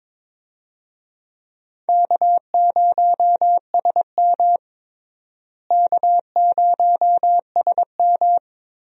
Morse code, heard as a keyed tone, about 21 wpm, 700 Hz. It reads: K0SM K0SM
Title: USA Call Signs - 4 Characters - Repeated 2x - 22wpm